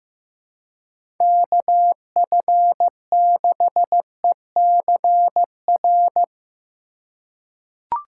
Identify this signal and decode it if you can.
Morse code, heard as a keyed tone, about 15 wpm, 700 Hz. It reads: KF6ECR
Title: USA Call Signs - 4x Character Spacing - 15wpm